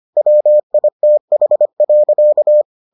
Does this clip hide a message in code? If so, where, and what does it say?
Yes: Morse code from 0.2 to 2.6 s, heard as a keyed tone, about 25 wpm, 600 Hz. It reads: WITH.